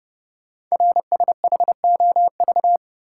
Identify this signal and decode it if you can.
Morse code, heard as a keyed tone, about 30 wpm, 700 Hz. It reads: RSHOV
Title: Code Groups - 5 Letter:Numbers - 30wpm